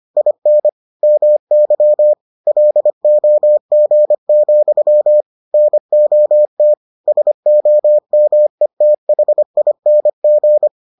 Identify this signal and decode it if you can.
Morse code, heard as a keyed tone, about 25 wpm, 600 Hz. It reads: IN MY LOG, NOT SOMETHING